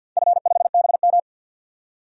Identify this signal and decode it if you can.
Morse code, heard as a keyed tone, about 51 wpm, 700 Hz. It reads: F5BM
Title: DXCC-100 Call Signs - 4 Characters - 50wpm